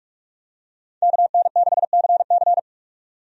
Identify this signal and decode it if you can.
Morse code, heard as a keyed tone, about 46 wpm, 700 Hz. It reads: KN6CC
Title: USA Call Signs - 5 Characters - 45wpm